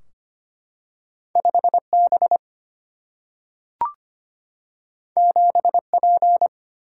Morse code, heard as a keyed tone, about 25 wpm, 700 Hz. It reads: 5 B 7 P
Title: Single-Letter Single-Number 25wpm